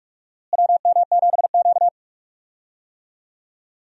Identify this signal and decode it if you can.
Morse code, heard as a keyed tone, about 46 wpm, 700 Hz. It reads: WM7Q
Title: USA Call Signs - 4 Characters - Repeated 3x - 45wpm